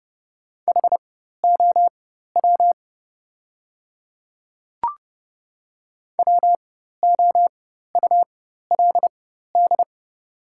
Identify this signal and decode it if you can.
Morse code, heard as a keyed tone, about 30 wpm, spaced as for 14 wpm, 700 Hz. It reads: HOW WOULD
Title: 4x Character Spacing - Top 200 Words - 30wpm